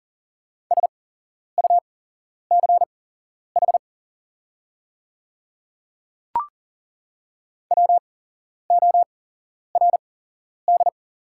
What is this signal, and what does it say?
Morse code, heard as a keyed tone, about 40 wpm, spaced as for 11 wpm, 690 Hz. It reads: SUCH WORD
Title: 8x Character Spacing - Top 300 Words - 40wpm